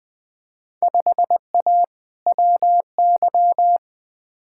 Morse code, heard as a keyed tone, about 20 wpm, 700 Hz. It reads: 5A WY
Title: Field Day Exchanges Encore - 20wpm